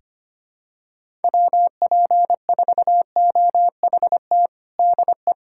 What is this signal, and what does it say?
Morse code, heard as a keyed tone, about 25 wpm, 700 Hz. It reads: WP4OHT DE
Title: USA Callsign DE Callsign 25wpm